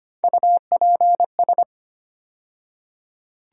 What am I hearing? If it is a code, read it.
Morse code, heard as a keyed tone, about 25 wpm, 700 Hz. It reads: UPS
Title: USA Call Sign Postfixes - 3 Letters 25wpm